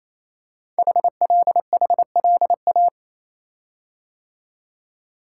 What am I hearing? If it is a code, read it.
Morse code, heard as a keyed tone, about 28 wpm, 700 Hz. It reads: HLHLA
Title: Code Groups - 5 Letters - 28wpm